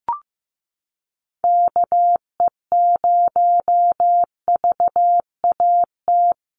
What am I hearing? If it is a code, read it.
Morse code, heard as a keyed tone, about 15 wpm, 700 Hz. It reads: KE0VAT
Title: USA Call Signs Repeated 2x 15wpm